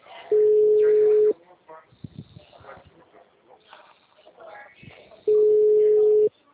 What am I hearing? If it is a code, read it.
Ringback tone (Europe, 425 Hz), with speech.